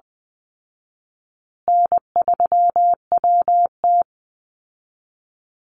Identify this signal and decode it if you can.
Morse code, heard as a keyed tone, about 20 wpm, 700 Hz. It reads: N3WT
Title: Easy Call Signs Repeated 2x - 20wpm